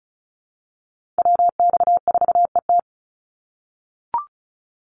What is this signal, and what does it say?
Morse code, heard as a keyed tone, about 34 wpm, 700 Hz. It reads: WX4ET
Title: NA SOTA Chaser Callsigns 35wpm